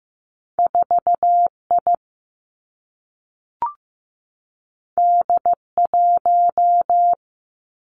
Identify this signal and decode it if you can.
Morse code, heard as a keyed tone, about 15 wpm, 700 Hz. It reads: 4 I D 1
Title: Two Challenging Letter-Number Combinations 15wpm